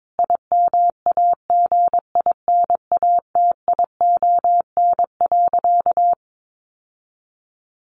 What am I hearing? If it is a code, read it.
Morse code, heard as a keyed tone, about 22 wpm, 700 Hz. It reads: IMAGINATION.
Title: Sentences from Top 1300 Words 22wpm